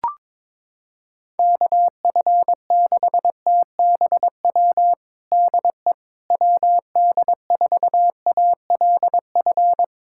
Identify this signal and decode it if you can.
Morse code, heard as a keyed tone, about 22 wpm, 700 Hz. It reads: KF6TBW DE WD4ALF
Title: USA Callsign DE Callsign - 6 Characters - 22wpm